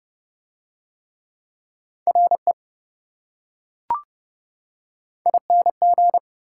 Morse code, heard as a keyed tone, about 26 wpm, 700 Hz. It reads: RE ING